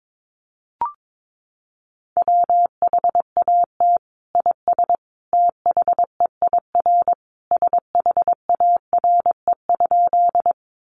Morse code, heard as a keyed tone, about 22 wpm, 700 Hz. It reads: WHAT IS THEIR SHARE?
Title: Sentences from Top 900 Words 22wpm